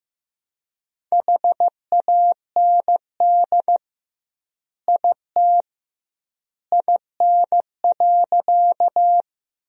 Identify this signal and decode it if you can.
Morse code, heard as a keyed tone, about 15 wpm, 700 Hz. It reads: HAND IT IN.
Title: 2x Word-Spacing - Sentences from Top 200 Words - 15wpm